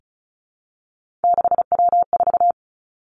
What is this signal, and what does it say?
Morse code, heard as a keyed tone, about 36 wpm, 700 Hz. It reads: 6W4
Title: Three Challenging Letter-Number Combinations 35wpm